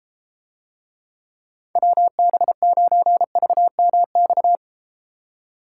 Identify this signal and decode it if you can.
Morse code, heard as a keyed tone, about 32 wpm, 700 Hz. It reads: WB9VMX